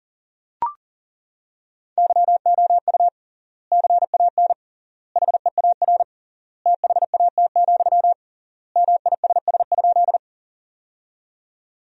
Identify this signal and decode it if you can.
Morse code, heard as a keyed tone, about 40 wpm, 700 Hz. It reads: YOU CAN HEAR THAT, MISS?
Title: 3x Word-Spacing - Sentences from Top 500 Words - 40wpm